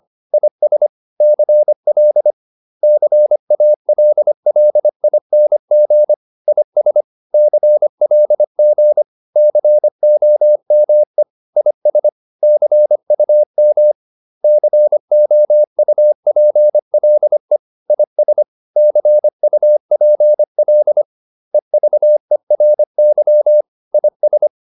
Morse code, heard as a keyed tone, about 25 wpm, 600 Hz. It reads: IS CL CALLING IS CLG COME IS CUM COUPLE IS CUPL EVERY IS